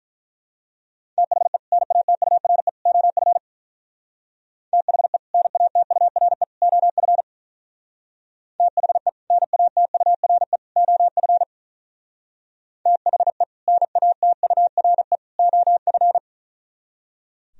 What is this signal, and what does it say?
Morse code, heard as a keyed tone, about 43 wpm, 700 Hz. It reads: THE NATURE OF THE NATURE OF THE NATURE OF THE NATURE OF
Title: Speed-Racing - Sets of 3 Words 35wpm